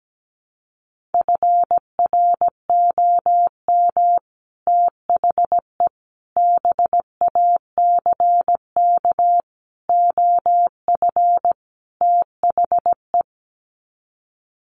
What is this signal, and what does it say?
Morse code, heard as a keyed tone, about 17 wpm, 700 Hz. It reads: FROM THE BACK OF THE
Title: Sets of 5 Words Encore 17wpm